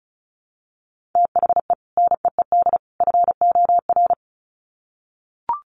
Morse code, heard as a keyed tone, about 34 wpm, 700 Hz. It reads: THE NEED FOR
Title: Sets of 3 Words 35wpm